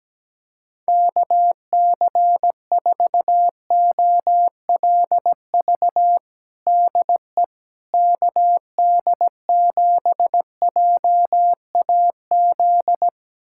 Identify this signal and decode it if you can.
Morse code, heard as a keyed tone, about 17 wpm, 700 Hz. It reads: KC4OLV DE KD7JAZ
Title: USA Callsign DE Callsign - 6 Characters - 17wpm